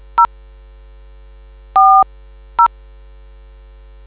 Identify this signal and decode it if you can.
DTMF, heard as telephone touch tones, tones of about 100 ms. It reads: 0 40